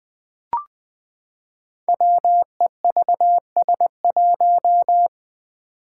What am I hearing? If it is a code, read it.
Morse code, heard as a keyed tone, about 20 wpm, 700 Hz. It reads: WEVS1
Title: Five Challenging Letter-Number Combinations 20wpm